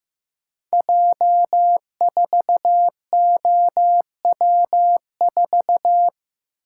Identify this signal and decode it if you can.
Morse code, heard as a keyed tone, about 15 wpm, 700 Hz. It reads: J4OW4
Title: Code Groups - 5 Letter:Numbers - 15wpm